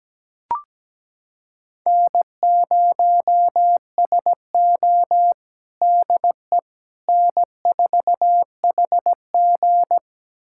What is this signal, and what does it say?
Morse code, heard as a keyed tone, about 17 wpm, 700 Hz. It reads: N0SO DE N4HG
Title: USA Callsign DE Callsign - 4 Characters - 17wpm